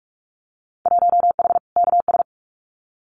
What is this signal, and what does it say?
Morse code, heard as a keyed tone, about 46 wpm, 700 Hz. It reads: 1H KS
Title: Winter Field Day Exchanges 45wpm